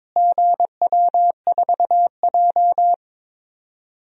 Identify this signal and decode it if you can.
Morse code, heard as a keyed tone, about 22 wpm, 700 Hz. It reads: GW4J
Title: DXCC-100 Call Signs - 4 Characters - Repeated 3x - 22wpm